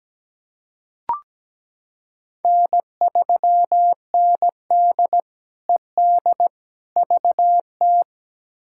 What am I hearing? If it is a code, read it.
Morse code, heard as a keyed tone, about 17 wpm, 700 Hz. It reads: N3ND ED VT
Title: K1USN SST QSO Practice - 17wpm